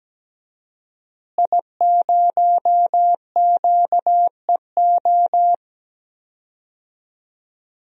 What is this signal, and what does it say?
Morse code, heard as a keyed tone, about 17 wpm, 700 Hz. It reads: I0QEO
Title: DXCC-100 Call Signs - 5 Characters - Repeated 3x - 17wpm